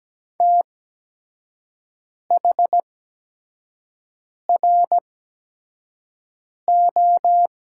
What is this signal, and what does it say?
Morse code, heard as a keyed tone, about 17 wpm, spaced as for 5 wpm, 700 Hz. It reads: THRO
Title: 8x Character Spacing - Top 500 Words - 17wpm